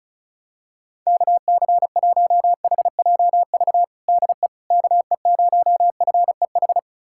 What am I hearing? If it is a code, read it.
Morse code, heard as a keyed tone, about 36 wpm, 700 Hz. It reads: KC1HJV DE KE0FEH
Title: USA Callsign DE Callsign 35wpm